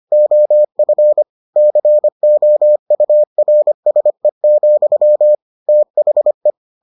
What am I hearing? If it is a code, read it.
Morse code, heard as a keyed tone, about 25 wpm, 600 Hz. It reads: OF COURSE, THE